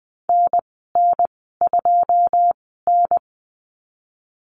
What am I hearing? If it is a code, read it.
Morse code, heard as a keyed tone, about 20 wpm, spaced as for 15 wpm, 700 Hz. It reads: NN2N
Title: 082 Lesson 14g - Number 2 - Call Signs 20wpm